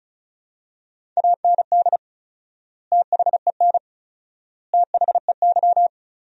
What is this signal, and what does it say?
Morse code, heard as a keyed tone, about 34 wpm, 700 Hz. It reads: AND THEN THEY
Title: 4x Word-Spacing - Sentences from Top 300 Words - 35wpm